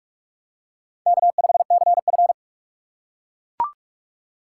Morse code, heard as a keyed tone, about 40 wpm, 700 Hz. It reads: K5CF